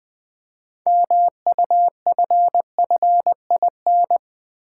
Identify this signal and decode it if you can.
Morse code, heard as a keyed tone, about 20 wpm, 700 Hz. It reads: MUFFIN
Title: Easy Words with Hints - 20wpm